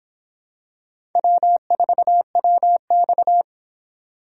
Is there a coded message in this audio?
Yes: Morse code from 1.2 to 3.4 s, heard as a keyed tone, about 24 wpm, 700 Hz. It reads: W4WX